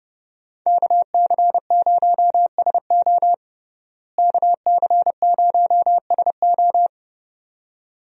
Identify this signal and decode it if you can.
Morse code, heard as a keyed tone, about 32 wpm, 700 Hz. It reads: KC0SO KC0SO
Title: USA Call Signs - 5 Characters - Repeated 2x - 30wpm